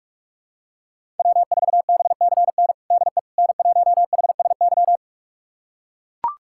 Morse code, heard as a keyed tone, about 42 wpm, 700 Hz. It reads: W4BCN DE N1HSY
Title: USA Callsign DE Callsign - 5 Characters - 45wpm